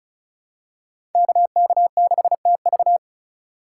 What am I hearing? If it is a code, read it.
Morse code, heard as a keyed tone, about 36 wpm, 700 Hz. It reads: KK6TV
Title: K1USN SST QSO Practice - 35wpm